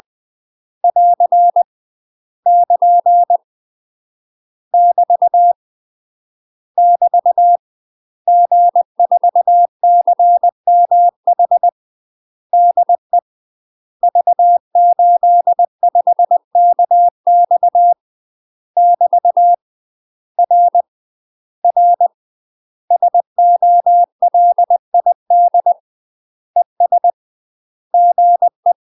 Morse code, heard as a keyed tone, about 20 wpm, 700 Hz. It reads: <AR> <KN> <BT> <BT> G4CMH DE V85KX <BT> R R SOLID ES GE